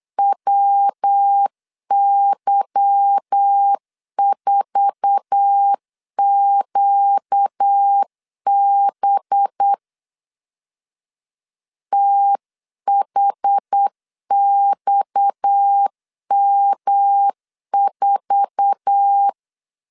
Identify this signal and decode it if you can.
Morse code, heard as a keyed tone, about 8 wpm, 800 Hz. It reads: WY4QB THXM4
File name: LEZIONI / 8 YJ45 / 8 YJ4 8wpm 3246